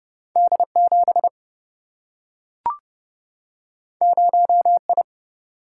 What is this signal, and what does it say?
Morse code, heard as a keyed tone, about 30 wpm, 700 Hz. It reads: D 7 0 I
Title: Single-Letter Single-Number 30wpm